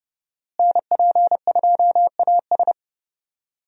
Morse code, heard as a keyed tone, about 30 wpm, 700 Hz. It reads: NP2AS
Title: USA Call Signs - 5 Characters - Repeated 3x - 30wpm